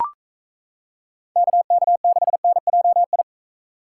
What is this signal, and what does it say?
Morse code, heard as a keyed tone, about 33 wpm, 700 Hz. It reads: KK6NJI
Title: Speed Racing - USA 2x3 Call Signs 28wpm